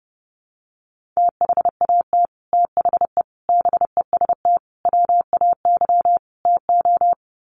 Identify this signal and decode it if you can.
Morse code, heard as a keyed tone, about 28 wpm, 700 Hz. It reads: THAT THE BEST WAY TO
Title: Sets of 5 Words Encore2 30wpm